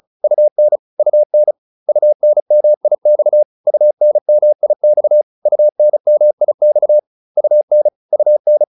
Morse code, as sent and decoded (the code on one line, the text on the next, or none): ..- -. / ..- -. / ..- -. -- .. -..- / ..- -. -- .. -..- / ..- -. -- .. -..- / ..- -. / ..- -.
UN UN UNMIX UNMIX UNMIX UN UN